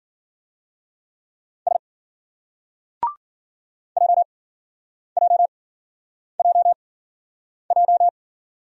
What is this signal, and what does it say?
Morse code, heard as a keyed tone, about 51 wpm, spaced as for 9 wpm, 700 Hz. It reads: S JJJJ